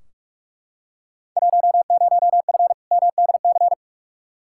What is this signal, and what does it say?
Morse code, heard as a keyed tone, about 41 wpm, 700 Hz. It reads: 10F MDC
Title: Field Day Exchanges Encore - 45wpm